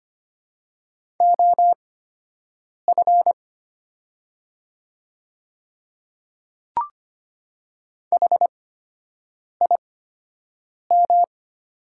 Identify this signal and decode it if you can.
Morse code, heard as a keyed tone, about 25 wpm, spaced as for 7 wpm, 690 Hz. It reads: OF HIM
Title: 8x Character Spacing - Top 200 Words - 25wpm